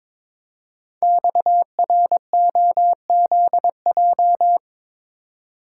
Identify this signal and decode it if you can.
Morse code, heard as a keyed tone, about 22 wpm, 700 Hz. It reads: XROZJ
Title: Code Groups - 5 Letters - 22wpm